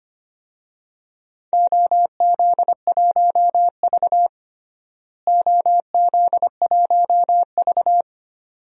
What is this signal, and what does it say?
Morse code, heard as a keyed tone, about 25 wpm, 700 Hz. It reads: OZ1V OZ1V
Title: DXCC-100 Call Signs - 4 Characters - Repeated 2x - 25wpm